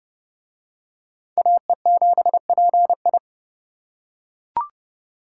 Morse code, heard as a keyed tone, about 31 wpm, 700 Hz. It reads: AE7PI